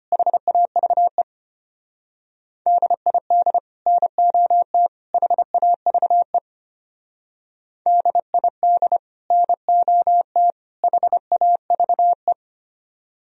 Morse code, heard as a keyed tone, about 32 wpm, 700 Hz. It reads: HAVE DID NOT HAVE DID NOT HAVE